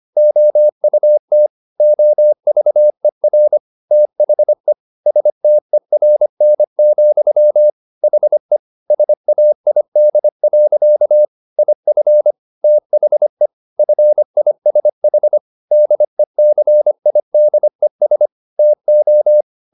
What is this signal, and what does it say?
Morse code, heard as a keyed tone, about 25 wpm, 600 Hz. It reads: OUT OVER THE STERN, HE SAID. IF THE FISH DECIDES TO
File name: the old man and the sea part 1 25wpm